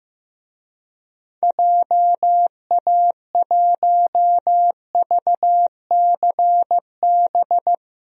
Morse code, heard as a keyed tone, about 16 wpm, 700 Hz. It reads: JA1VCB
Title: DXCC-100 Call Signs - 6 Characters - 15wpm